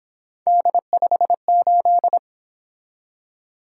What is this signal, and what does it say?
Morse code, heard as a keyed tone, about 27 wpm, 700 Hz. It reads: D58
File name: Speed-Racing - Three Letter-Number Combinations - 22wpm